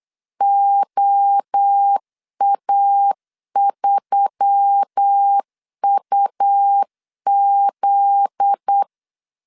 Morse code, heard as a keyed tone, about 8 wpm, 800 Hz. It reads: OA3UZ
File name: LEZIONI / 8 YJ45 / 8 YJ45 8wpm 3214